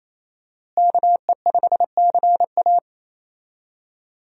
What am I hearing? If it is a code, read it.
Morse code, heard as a keyed tone, about 28 wpm, 700 Hz. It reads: KE5CA
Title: USA Call Signs - 5 Characters - Repeated 2x - 28wpm